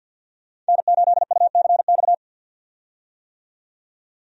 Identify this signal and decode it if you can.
Morse code, heard as a keyed tone, about 50 wpm, 700 Hz. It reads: N8UCX
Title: USA Call Signs - 5 Characters - 50wpm